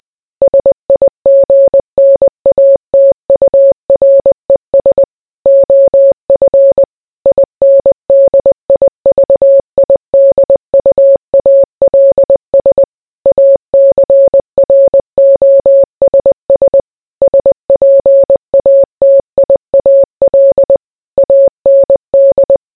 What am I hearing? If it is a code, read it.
Morse code, heard as a keyed tone, about 20 wpm, 550 Hz. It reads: SIGNATURES OF INDIVIDUALS ACROSS SPATIAL AND